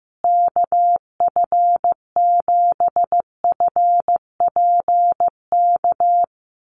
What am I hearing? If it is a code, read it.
Morse code, heard as a keyed tone, about 15 wpm, 700 Hz. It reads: KF7FPK
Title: USA Call Signs Repeated 2x 15wpm